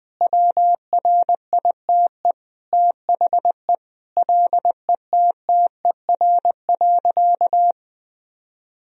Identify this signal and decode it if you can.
Morse code, heard as a keyed tone, about 20 wpm, 700 Hz. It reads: WRITE THE LETTER.
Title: Sentences from Top 1000 Words - Review Part 3 - 20wpm